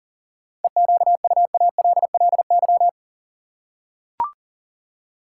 Morse code, heard as a keyed tone, about 37 wpm, 700 Hz. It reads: EQUALLY
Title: Seven Letter Words - 40wpm